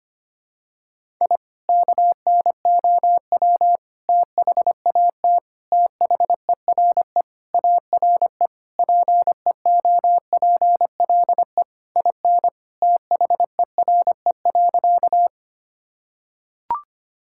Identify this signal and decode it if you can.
Morse code, heard as a keyed tone, about 25 wpm, 700 Hz. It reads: I KNOW THAT THERE ARE PEOPLE IN THERE.
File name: practice / 25wpm / Sentences with 3-Word Phrases - 25wpm